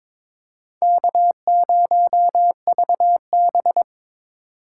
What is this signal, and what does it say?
Morse code, heard as a keyed tone, about 22 wpm, 700 Hz. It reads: K0VB
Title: USA Call Signs - 4 Characters - 8x Character Spacing - 22wpm